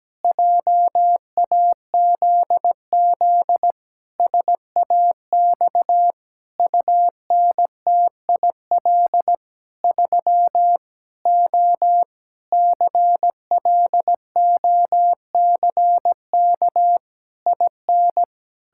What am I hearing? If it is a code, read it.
Morse code, heard as a keyed tone, about 17 wpm, 700 Hz. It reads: JAZZ SAX UNTIL 3 O CLOCK IN
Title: Pangram Challenge v2 - 17wpm